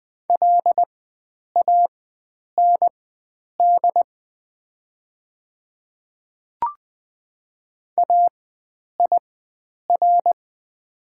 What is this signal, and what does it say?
Morse code, heard as a keyed tone, about 20 wpm, spaced as for 9 wpm, 700 Hz. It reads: LAND AIR
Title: 4x Character Spacing - Top 500 Words - 20wpm